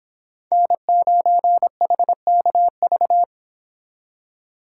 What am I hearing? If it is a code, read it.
Morse code, heard as a keyed tone, about 26 wpm, 700 Hz. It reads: N9HKV